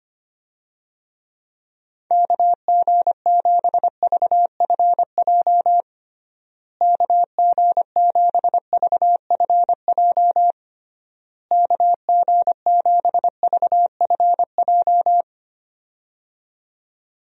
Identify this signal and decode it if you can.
Morse code, heard as a keyed tone, about 25 wpm, 700 Hz. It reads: KG7VFJ KG7VFJ KG7VFJ
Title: USA Call Signs - 6 Characters - Repeated 3x - 25wpm